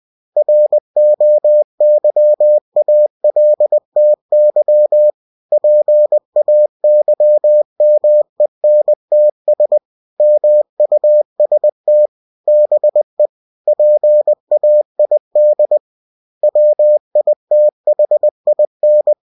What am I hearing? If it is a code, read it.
Morse code, heard as a keyed tone, about 20 wpm, 600 Hz. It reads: ROYALTY PAYMENTS MUST BE PAID WITHIN
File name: cw-books / winnie-the-pooh / winnie-20wpm-page-0117